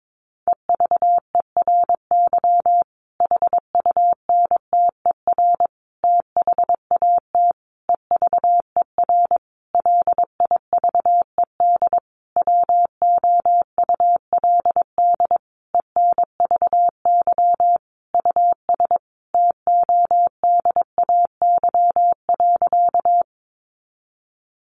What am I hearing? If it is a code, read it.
Morse code, heard as a keyed tone, about 22 wpm, 700 Hz. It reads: EVERY HUNTER THAT EVER LIVED WOULD ENVY US TODAY.
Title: A Sound Of Thunder 22wpm